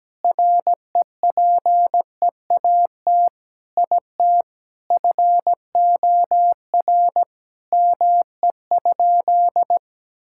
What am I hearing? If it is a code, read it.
Morse code, heard as a keyed tone, about 17 wpm, 700 Hz. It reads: REPEAT IT FOR ME?
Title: Sentences from Top 1600 Words 17wpm